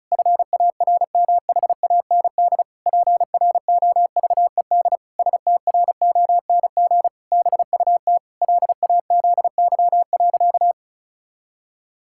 Morse code, heard as a keyed tone, about 36 wpm, 700 Hz. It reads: FARMHAND PROVED STRONG BUT LAZY.
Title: Pangram Challenge v2 - 35wpm